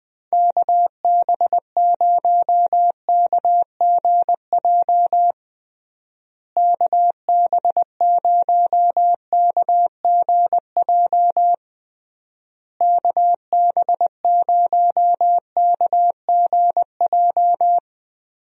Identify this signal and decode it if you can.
Morse code, heard as a keyed tone, about 20 wpm, 700 Hz. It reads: KB0KGJ KB0KGJ KB0KGJ
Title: USA Call Signs - 6 Characters - Repeated 3x - 20wpm